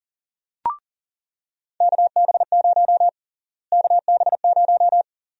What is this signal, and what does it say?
Morse code, heard as a keyed tone, about 40 wpm, 700 Hz. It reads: KB0 KB0
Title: USA Call Sign Prefixes - Repeated 2x - 40wpm